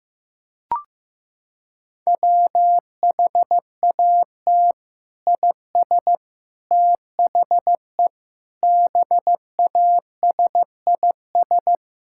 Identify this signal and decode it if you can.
Morse code, heard as a keyed tone, about 15 wpm, 700 Hz. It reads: WHAT IS THE BASIS
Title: Sentences from Top 1000 Words - Review Part 6 - 15wpm